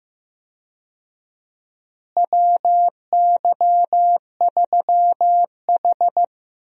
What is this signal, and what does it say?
Morse code, heard as a keyed tone, about 15 wpm, 700 Hz. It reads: WY3H
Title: USA Call Signs - 4 Characters - Repeated 2x - 15wpm